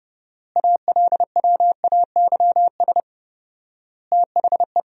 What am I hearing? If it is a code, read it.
Morse code, heard as a keyed tone, about 32 wpm, 700 Hz. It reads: ALWAYS THE
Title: 4x Word-Spacing - Sentences from Top 300 Words - 30wpm